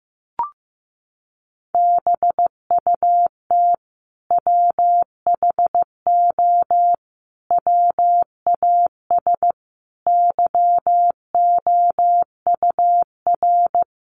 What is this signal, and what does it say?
Morse code, heard as a keyed tone, about 15 wpm, 700 Hz. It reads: BUT WHO WAS YOUR
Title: Sentences from Top 200 Words 15wpm